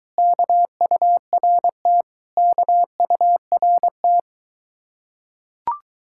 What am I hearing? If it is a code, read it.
Morse code, heard as a keyed tone, about 22 wpm, 700 Hz. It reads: KURT KURT